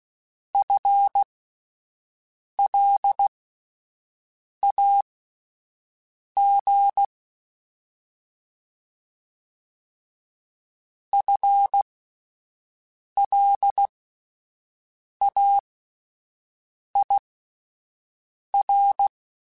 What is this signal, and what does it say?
Morse code, heard as a keyed tone, about 16 wpm, spaced as for 5 wpm, 800 Hz. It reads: FLAG FLAIR
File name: Spaced / FL Runs sp mp3s / FL Runs sp @10